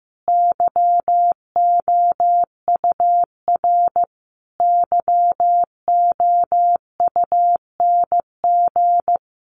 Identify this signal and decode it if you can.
Morse code, heard as a keyed tone, about 14 wpm, 700 Hz. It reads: YOUR YOUNG
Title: Sentences from Top 1300 Words 15wpm